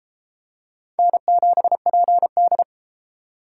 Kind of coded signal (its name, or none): Morse code